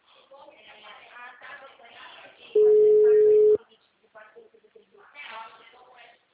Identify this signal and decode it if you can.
Ringback tone (Europe, 425 Hz), with speech.